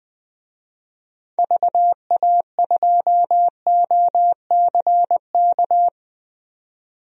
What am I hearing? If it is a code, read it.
Morse code, heard as a keyed tone, about 20 wpm, 700 Hz. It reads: VA2OCK
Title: DXCC-100 Call Signs - 6 Characters - 20wpm